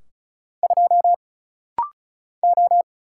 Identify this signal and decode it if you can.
Morse code, heard as a keyed tone, about 33 wpm, 700 Hz. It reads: 2 O
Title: Single Letter-Number - Rapid-Fire 35wpm